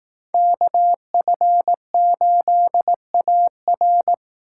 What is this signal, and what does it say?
Morse code, heard as a keyed tone, about 20 wpm, 700 Hz. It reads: KF8AR